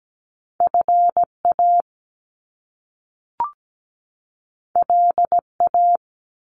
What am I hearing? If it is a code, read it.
Morse code, heard as a keyed tone, about 17 wpm, 700 Hz. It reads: FA LA
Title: Top Two Letter Combinations in Words 17wpm